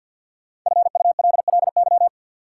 Morse code, heard as a keyed tone, about 53 wpm, 700 Hz. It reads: FULLY
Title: Sentences from Top 1300 Words - New Word Drill - 50wpm